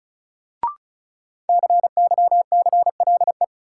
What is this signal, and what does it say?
Morse code, heard as a keyed tone, about 34 wpm, 700 Hz. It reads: CYCLE
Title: Sentences from Top 900 Words - New Word Drill - 35wpm